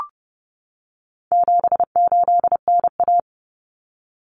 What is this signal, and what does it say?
Morse code, heard as a keyed tone, about 30 wpm, 700 Hz. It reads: 78NA
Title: Common Cut Numbers - 4-Digit 30wpm